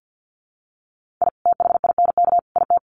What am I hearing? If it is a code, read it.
Morse code, heard as a keyed tone, about 48 wpm, 700 Hz. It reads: I THINK IT
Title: Sets of 3 Words 50wpm